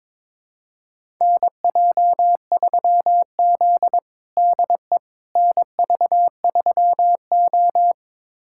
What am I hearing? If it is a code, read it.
Morse code, heard as a keyed tone, about 22 wpm, 700 Hz. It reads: NJ3Z DE NV3O
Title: USA Callsign DE Callsign - 4 Characters - 22wpm